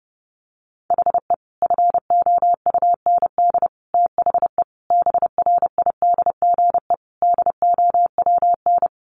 Morse code, heard as a keyed tone, about 28 wpm, 700 Hz. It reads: HE FOUND THE BRIDGE DOWN
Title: Sentences from Top 1200 Words 30wpm